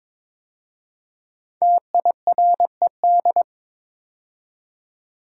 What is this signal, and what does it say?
Morse code, heard as a keyed tone, about 22 wpm, 700 Hz. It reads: TIRED
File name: 5 Letter Words - First 3 Letters Given - 22wpm